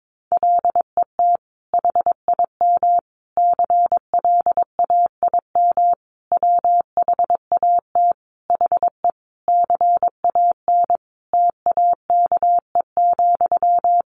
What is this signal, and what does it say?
Morse code, heard as a keyed tone, about 22 wpm, 700 Hz. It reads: LET HIM CLAIM WHAT HE CAN TAKE,
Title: Sentences from Top 1500 Words 22wpm